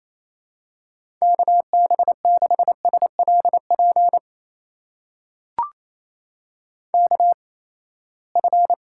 Morse code, heard as a keyed tone, about 28 wpm, spaced as for 19 wpm, 700 Hz. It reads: KB6SLP KF
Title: USA Call Signs - 6 Characters - 8x Character Spacing - 28wpm